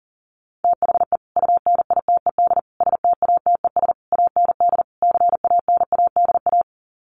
Morse code, heard as a keyed tone, about 37 wpm, 700 Hz. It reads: THE UNITED STATES AND CANADA
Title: Sets of 5 Words 40wpm